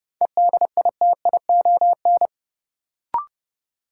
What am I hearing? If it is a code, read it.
Morse code, heard as a keyed tone, about 30 wpm, 700 Hz. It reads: EDITION
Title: Sets of 5 Words - New Word Drill - 30wpm